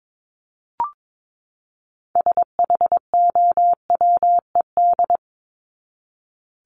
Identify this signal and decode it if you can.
Morse code, heard as a keyed tone, about 22 wpm, 700 Hz. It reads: SHOWED
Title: ED Suffix - Top 100 Words with Base Words 22wpm